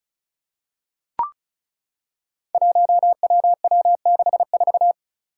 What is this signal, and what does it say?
Morse code, heard as a keyed tone, about 36 wpm, 700 Hz. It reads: 1WW64
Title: Five Challenging Letter-Number Combinations 35wpm